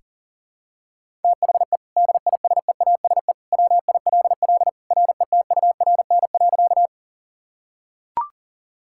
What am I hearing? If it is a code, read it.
Morse code, heard as a keyed tone, about 40 wpm, 700 Hz. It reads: THE DISEASE WILL RETURN.
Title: Sentences from Top 1000 Words - Review Part 6 - 40wpm